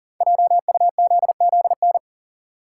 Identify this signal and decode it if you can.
Morse code, heard as a keyed tone, about 40 wpm, 700 Hz. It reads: JUZZN
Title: Code Groups - 5 Letters - 40wpm